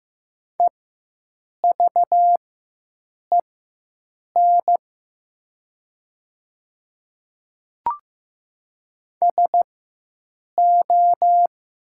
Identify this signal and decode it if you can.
Morse code, heard as a keyed tone, about 15 wpm, spaced as for 7 wpm, 700 Hz. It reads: EVEN SO
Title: 4x Character Spacing - Top 200 Words - 15wpm